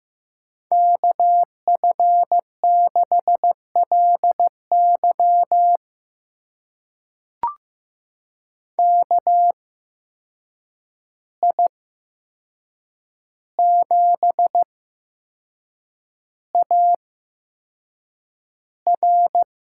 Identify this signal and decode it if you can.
Morse code, heard as a keyed tone, about 15 wpm, spaced as for 6 wpm, 700 Hz. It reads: KF6LY KI7AR
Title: USA Call Signs - 8x Character Spacing - 15wpm